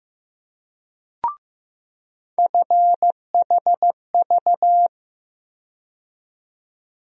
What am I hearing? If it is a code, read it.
Morse code, heard as a keyed tone, about 15 wpm, 700 Hz. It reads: FHV
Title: USA Call Sign Postfixes - 3 Letters 15wpm